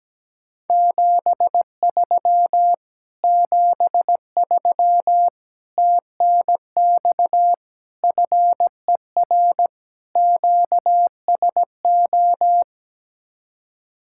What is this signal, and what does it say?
Morse code, heard as a keyed tone, about 17 wpm, 700 Hz. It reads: 73 73 TNX FER QSO
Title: Speed-Racing Common QSO Phrases Encore - 17wpm